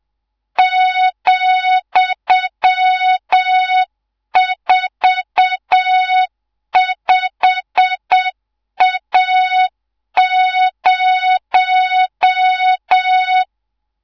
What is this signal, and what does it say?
Morse code, heard as a keyed tone, about 7 wpm, 3700 Hz, with noise. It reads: ,45A0
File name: Morze-HA2MN / hangp / 2 03 GYAKORLAT